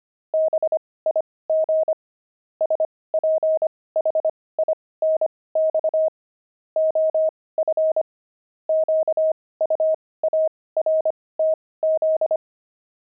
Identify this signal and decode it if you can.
Morse code, heard as a keyed tone, about 25 wpm, spaced as for 18 wpm, 630 Hz. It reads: BIG SPHINX OF QUARTZ
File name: pangram-fs-18-cs-25-jackdaws